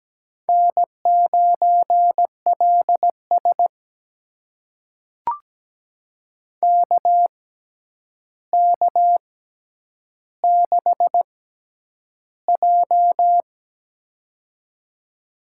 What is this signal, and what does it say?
Morse code, heard as a keyed tone, about 17 wpm, spaced as for 9 wpm, 700 Hz. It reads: N9LS KK6J
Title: USA Call Signs - 4 Characters - 6x Character Spacing - 17wpm